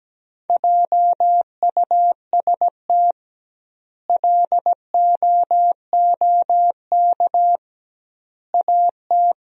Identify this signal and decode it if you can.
Morse code, heard as a keyed tone, about 17 wpm, 700 Hz. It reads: JUST LOOK AT
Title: 2x Word-Spacing - Sentences from Top 200 Words - 17wpm